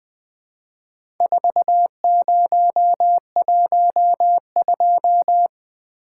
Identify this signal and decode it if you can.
Morse code, heard as a keyed tone, about 20 wpm, 700 Hz. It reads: 4012
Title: Four Letter-Number Combinations - 20wpm